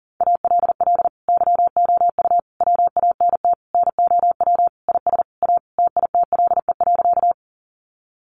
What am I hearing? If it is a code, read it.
Morse code, heard as a keyed tone, about 40 wpm, 700 Hz. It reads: ALL YOU WANT NOW IS A TITLE.
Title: Sentences from Top 1200 Words 40wpm